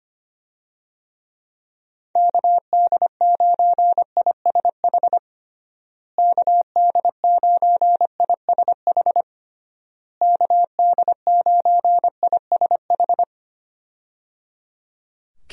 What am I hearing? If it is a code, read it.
Morse code, heard as a keyed tone, about 25 wpm, 700 Hz. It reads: KD9ISH KD9ISH KD9ISH
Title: USA Call Signs - 6 Characters - Repeated 3x - 25wpm